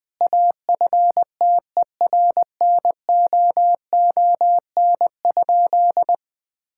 Morse code, heard as a keyed tone, about 20 wpm, 700 Hz. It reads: AFTERNOON?
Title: Sentences from Top 1600 Words 20wpm